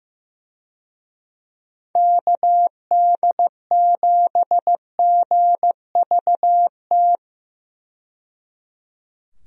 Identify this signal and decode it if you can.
Morse code, heard as a keyed tone, about 15 wpm, 700 Hz. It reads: KD7GVT
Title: USA Call Signs - 6 Characters - 15wpm